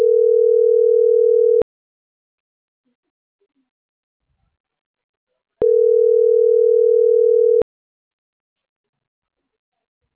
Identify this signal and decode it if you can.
Ringback tone (North America), 440 + 480 Hz.
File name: CENTRAL MAYAB / out atzhyry.gomez